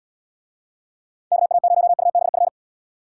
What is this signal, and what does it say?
Morse code, heard as a keyed tone, about 79 wpm, 700 Hz. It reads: KA0UBF